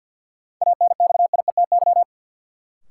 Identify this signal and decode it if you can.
Morse code, heard as a keyed tone, about 51 wpm, 700 Hz. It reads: ANXIETY
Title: Sentences from Top 1600 Words - New Word Drill - 50wpm